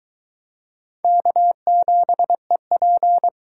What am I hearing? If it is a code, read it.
Morse code, heard as a keyed tone, about 21 wpm, 700 Hz. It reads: K7EP